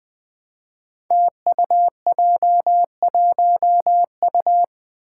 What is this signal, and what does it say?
Morse code, heard as a keyed tone, about 21 wpm, 700 Hz. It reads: TUJ1U
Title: Five Challenging Letter-Number Combinations 20wpm